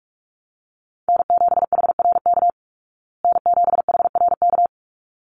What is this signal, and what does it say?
Morse code, heard as a keyed tone, about 44 wpm, 700 Hz. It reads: N7HRK N7HRK
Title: Call Signs Repeated 3x - 45wpm